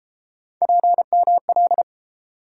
Morse code, heard as a keyed tone, about 31 wpm, 700 Hz. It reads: PML